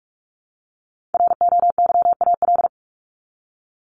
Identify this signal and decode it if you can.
Morse code, heard as a keyed tone, about 44 wpm, 700 Hz. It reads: ROYAL
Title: Top 1751-2000 Words 45wpm